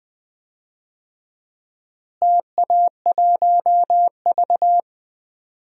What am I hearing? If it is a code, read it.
Morse code, heard as a keyed tone, about 20 wpm, 700 Hz. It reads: TA1V
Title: DXCC-100 Call Signs - 4 Characters - Repeated 3x - 20wpm